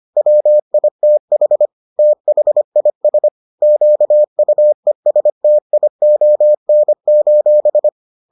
Morse code, heard as a keyed tone, about 25 wpm, 600 Hz. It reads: WITH THIS QUESTION: